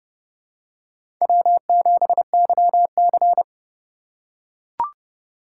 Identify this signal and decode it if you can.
Morse code, heard as a keyed tone, about 30 wpm, 700 Hz. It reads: W7YC
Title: USA Call Signs - 4 Characters - 3x Character Spacing - 30wpm